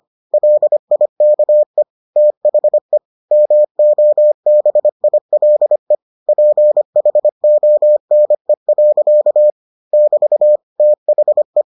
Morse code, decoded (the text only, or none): LIKE THE MOBILE PHONE. <BT> THE